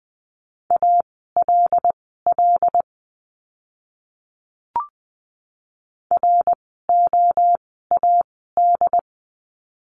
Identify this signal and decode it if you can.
Morse code, heard as a keyed tone, about 20 wpm, spaced as for 14 wpm, 700 Hz. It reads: ALL ROAD
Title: 101 Lesson 17f - Letter u - Words 20wpm